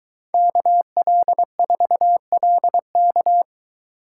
Morse code, heard as a keyed tone, about 22 wpm, 700 Hz. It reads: KL4LK